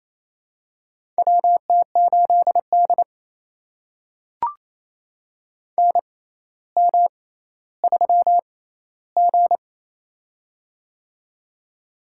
Morse code, heard as a keyed tone, about 27 wpm, spaced as for 14 wpm, 700 Hz. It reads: WT8D NM3G
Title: USA Call Signs - 4 Characters - 6x Character Spacing - 28wpm